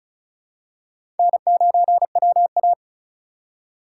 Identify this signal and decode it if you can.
Morse code, heard as a keyed tone, about 36 wpm, 700 Hz. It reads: N9WA
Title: USA Call Signs - 4 Characters - 3x Character Spacing - 35wpm